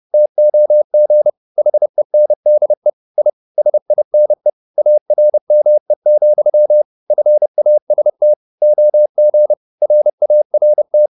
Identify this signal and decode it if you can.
Morse code, heard as a keyed tone, about 30 wpm, 600 Hz. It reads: TOG HENDE I SINE ARME, FAST OG RART
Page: Ferie 30 wpm